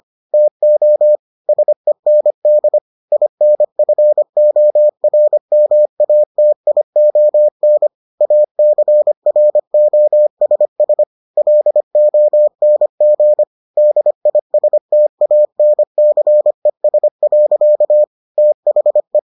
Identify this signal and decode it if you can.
Morse code, heard as a keyed tone, about 25 wpm, 600 Hz. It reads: TO SEND INFORMATION ACROSS LONG DISTANCES. THE